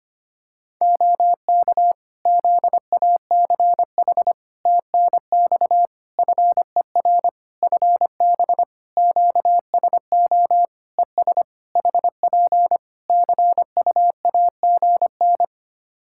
Morse code, heard as a keyed tone, about 25 wpm, 700 Hz. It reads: OK ZACH TNX FER FB QSO ES HP CUAGN
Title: K1USN SST QSO Practice 25wpm